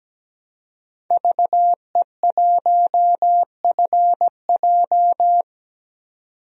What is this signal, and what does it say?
Morse code, heard as a keyed tone, about 17 wpm, 700 Hz. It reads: VE1FJ
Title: DXCC-100 Call Signs - 5 Characters - 17wpm